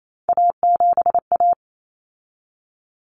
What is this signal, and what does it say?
Morse code, heard as a keyed tone, about 28 wpm, 700 Hz. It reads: A7A
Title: Common Cut Numbers - 3-Digit 28wpm